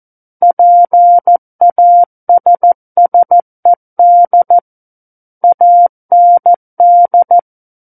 Morse code, heard as a keyed tone, about 14 wpm, 700 Hz. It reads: PASSED AND
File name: Book One / 05-Chapter 5 - 14 WPM